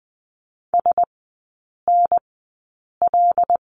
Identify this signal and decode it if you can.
Morse code, heard as a keyed tone, about 20 wpm, spaced as for 8 wpm, 700 Hz. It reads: SNL
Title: 074 Lesson 13e - Letter L - Three Characters 20wpm